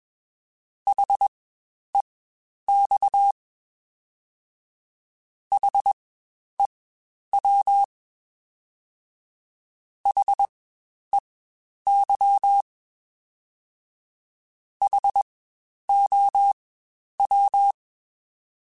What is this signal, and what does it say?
Morse code, heard as a keyed tone, about 21 wpm, spaced as for 10 wpm, 800 Hz. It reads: HEX HEW HEY HOW
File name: Spaced Alphas / Spaced Alphas mp3s / Hs sp @20